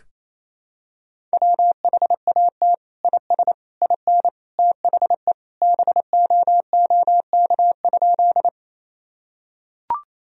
Morse code, heard as a keyed tone, about 28 wpm, 700 Hz. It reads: WHAT IS IN THE BOOK?
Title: Sentences from Top 1000 Words - Review Part 1 - 28wpm